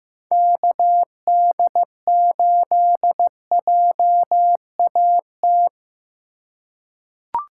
Morse code, heard as a keyed tone, about 15 wpm, 700 Hz. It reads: KD8JAT
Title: USA Call Signs - 6 Characters - 8x Character Spacing - 15wpm